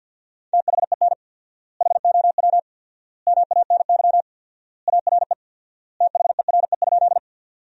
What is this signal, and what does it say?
Morse code, heard as a keyed tone, about 50 wpm, 700 Hz. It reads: THEN HOW MANY ARE THERE?
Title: 4x Word-Spacing - Sentences from Top 100 Words - 50wpm